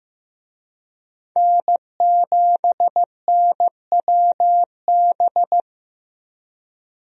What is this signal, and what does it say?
Morse code, heard as a keyed tone, about 15 wpm, 700 Hz. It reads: N7NWB
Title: USA Call Signs - 5 Characters - Repeated 2x - 15wpm